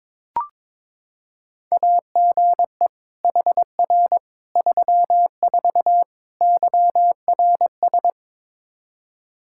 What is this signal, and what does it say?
Morse code, heard as a keyed tone, about 22 wpm, 700 Hz. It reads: AGE HR 34 YRS
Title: Practice QSO - Exchange 3 - More Info - 22wpm